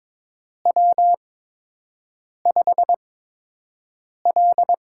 Morse code, heard as a keyed tone, about 22 wpm, spaced as for 6 wpm, 700 Hz. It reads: W5L
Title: USA Call Signs - 8x Character Spacing - 22wpm